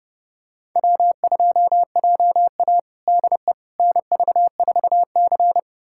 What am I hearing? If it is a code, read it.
Morse code, heard as a keyed tone, about 30 wpm, 700 Hz. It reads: W2JA DE NV4C
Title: USA Callsign DE Callsign - 4 Characters - 30wpm